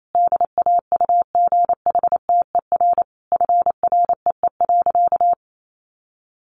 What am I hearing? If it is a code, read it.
Morse code, heard as a keyed tone, about 28 wpm, 700 Hz. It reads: DAUGHTER FREE.
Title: Sentences from Top 1500 Words 28wpm